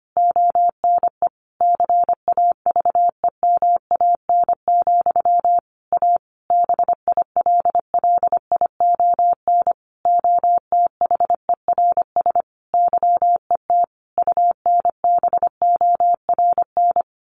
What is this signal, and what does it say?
Morse code, heard as a keyed tone, about 25 wpm, 700 Hz. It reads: ONE CAVEMAN, A BILLION OTHERS YET UNBORN
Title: A Sound Of Thunder 25wpm